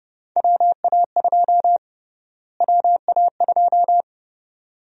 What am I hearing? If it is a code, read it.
Morse code, heard as a keyed tone, about 30 wpm, 700 Hz. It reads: WA2 WA2
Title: USA Call Sign Prefixes - Repeated 2x - 30wpm